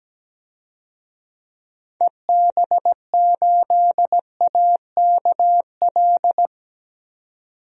Morse code, heard as a keyed tone, about 17 wpm, 700 Hz. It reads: EB8AKL
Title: DXCC-100 Call Signs - 6 Characters - 17wpm